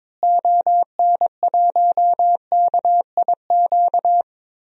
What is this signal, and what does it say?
Morse code, heard as a keyed tone, about 21 wpm, 700 Hz. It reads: ON1KIQ
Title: DXCC-100 Call Signs - 6 Characters - Repeated 2x - 22wpm